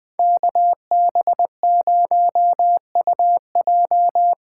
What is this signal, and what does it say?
Morse code, heard as a keyed tone, about 20 wpm, 700 Hz. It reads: KB0UJ